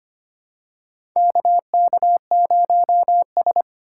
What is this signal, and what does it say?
Morse code, heard as a keyed tone, about 25 wpm, 700 Hz. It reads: KK0S
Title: USA Call Signs - 4 Characters - 4x Character Spacing - 25wpm